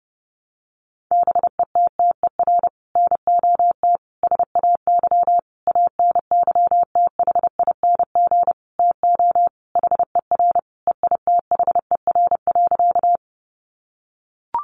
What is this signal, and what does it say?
Morse code, heard as a keyed tone, about 30 wpm, 700 Hz. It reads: BETTER NOT SAY ANYTHING TO HER EITHER.
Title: Sentences from Top 1000 Words 30wpm